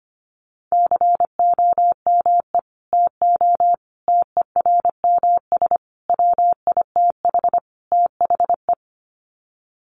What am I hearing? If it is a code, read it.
Morse code, heard as a keyed tone, about 25 wpm, 700 Hz. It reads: COME TO TERMS WITH THE
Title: Sets of 5 Words Encore2 25wpm